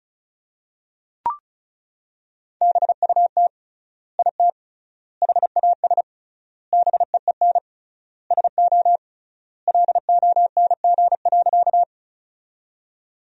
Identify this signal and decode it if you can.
Morse code, heard as a keyed tone, about 36 wpm, 700 Hz. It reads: BUT IT HAS BEEN SO LONG.
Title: 3x Word-Spacing - Sentences from Top 100 Words - 35wpm